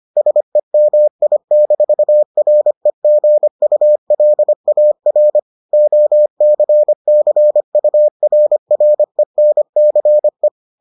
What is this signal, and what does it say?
Morse code, heard as a keyed tone, about 25 wpm, 600 Hz. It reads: SEMI-REGULAR OCCURRENCE